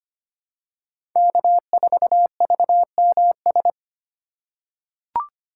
Morse code, heard as a keyed tone, about 25 wpm, 700 Hz. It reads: K4VMS
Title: USA Call Signs - 5 Characters - 2x Character Spacing - 25wpm